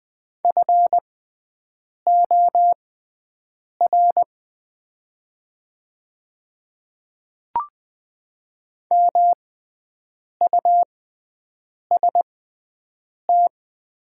Morse code, heard as a keyed tone, about 20 wpm, spaced as for 7 wpm, 700 Hz. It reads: FOR MUST
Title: 6x Character Spacing - Top 300 Words - 20wpm